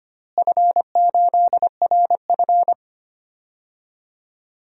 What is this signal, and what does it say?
Morse code, heard as a keyed tone, about 25 wpm, 700 Hz. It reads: F8RF
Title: DXCC-100 Call Signs - 4 Characters - Repeated 3x - 25wpm